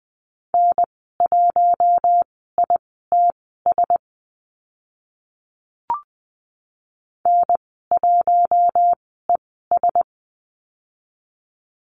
Morse code, heard as a keyed tone, about 20 wpm, spaced as for 14 wpm, 700 Hz. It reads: N1ITS N1ES
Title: 076 Lesson 13g - Letter L - Call Signs 20wpm